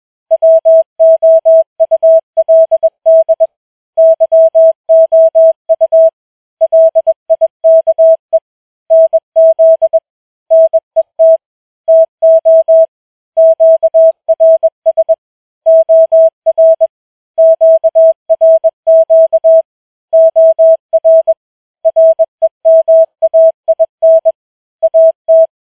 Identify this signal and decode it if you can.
Morse code, heard as a keyed tone, about 21 wpm, 650 Hz. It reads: WOULD YOU LIKE NZ NET TO QRS OR QRQ OR REMAIN AT
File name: challenge-184